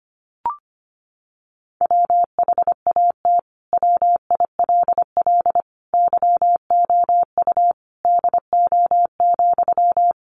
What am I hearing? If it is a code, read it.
Morse code, heard as a keyed tone, about 25 wpm, 700 Hz. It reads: WHAT WILL YOU DO,
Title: Sentences from Top 100 Words 25wpm